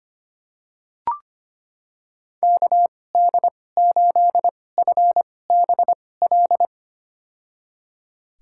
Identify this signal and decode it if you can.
Morse code, heard as a keyed tone, about 25 wpm, spaced as for 18 wpm, 700 Hz. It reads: KD8FBL
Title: USA Call Signs - 6 Characters - 2x Character Spacing - 25wpm